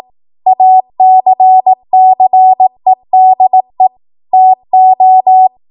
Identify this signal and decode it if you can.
Morse code, heard as a keyed tone, about 18 wpm, 750 Hz. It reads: ACCEDE TO
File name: Archive / 18 WPM / 18WPM